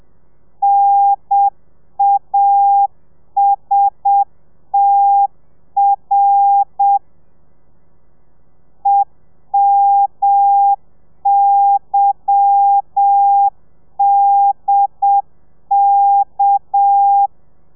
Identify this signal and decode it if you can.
Morse code, heard as a keyed tone, about 7 wpm, 800 Hz. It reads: NASTR EMYDK
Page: Lezione 1: audio a 30 CPM